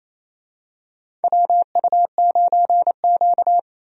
Morse code, heard as a keyed tone, about 28 wpm, 700 Hz. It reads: WU9Q
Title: USA Call Signs - 4 Characters - Repeated 2x - 28wpm